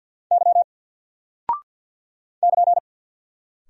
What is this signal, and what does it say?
Morse code, heard as a keyed tone, about 48 wpm, 700 Hz. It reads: Y <KN>
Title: Single Character (Letter:Number:Punctuation:Prosign) 50wpm